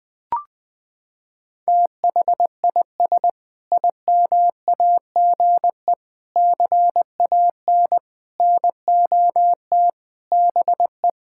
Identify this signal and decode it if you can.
Morse code, heard as a keyed tone, about 20 wpm, 700 Hz. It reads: THIS IMAGE CAN NOT BE
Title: Sentences with 5-Word Phrases - 20wpm